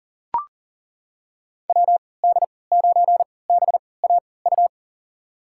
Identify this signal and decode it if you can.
Morse code, heard as a keyed tone, about 40 wpm, spaced as for 23 wpm, 700 Hz. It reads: WD9BAU
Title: USA Call Signs - 6 Characters - 3x Character Spacing - 40wpm